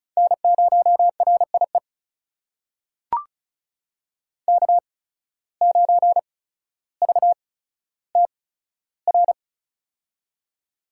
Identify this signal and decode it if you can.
Morse code, heard as a keyed tone, about 36 wpm, spaced as for 15 wpm, 700 Hz. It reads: N0RIE K9VTR
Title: USA Call Signs - 5 Characters - 8x Character Spacing - 35wpm